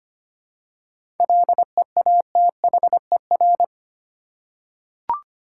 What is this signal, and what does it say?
Morse code, heard as a keyed tone, about 25 wpm, 700 Hz. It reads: LEATHER
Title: Seven Letter Words - 25wpm